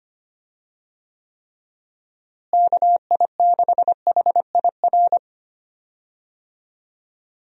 Morse code, heard as a keyed tone, about 25 wpm, 700 Hz. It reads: KI6HIR
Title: USA Call Signs - 6 Characters - Repeated 3x - 25wpm